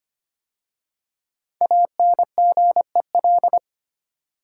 Morse code, heard as a keyed tone, about 25 wpm, 700 Hz. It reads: ANGEL
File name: 5 Letter Words - First 3 Letters Given - 25wpm